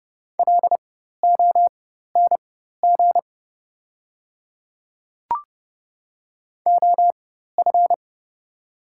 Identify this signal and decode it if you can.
Morse code, heard as a keyed tone, about 29 wpm, spaced as for 14 wpm, 700 Hz. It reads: LONG OF
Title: 4x Character Spacing - Top 100 Words - 30wpm